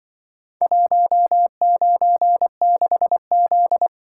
Morse code, heard as a keyed tone, about 23 wpm, 700 Hz. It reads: 196Z
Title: Speed-Racing - Four Letter-Number Combinations - 20wpm